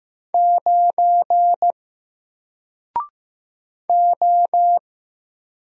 Morse code, heard as a keyed tone, about 15 wpm, 700 Hz. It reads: 9 O
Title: Single Letter-Number 15wpm